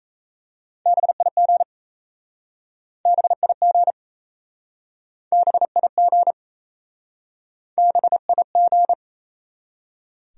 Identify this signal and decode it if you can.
Morse code, heard as a keyed tone, about 37 wpm, 700 Hz. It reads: BIG BIG BIG BIG